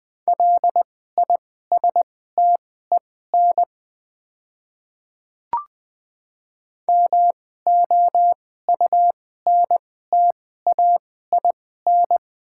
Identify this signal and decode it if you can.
Morse code, heard as a keyed tone, about 20 wpm, spaced as for 14 wpm, 700 Hz. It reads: LISTEN MOUNTAIN
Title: 2x Character Spacing - Top 500 Words - 20wpm